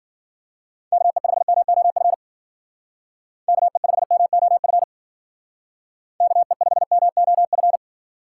Morse code, heard as a keyed tone, about 49 wpm, 700 Hz. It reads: KE5MOF KE5MOF KE5MOF
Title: Speed Racing - USA Call Signs 40wpm